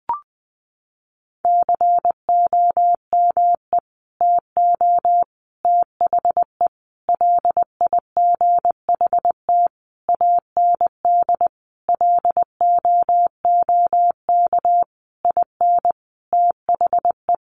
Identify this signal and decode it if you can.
Morse code, heard as a keyed tone, about 20 wpm, 700 Hz. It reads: COME TO THE LIGHT AND LOOK IN THE
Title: Sentences from Top 1300 Words 20wpm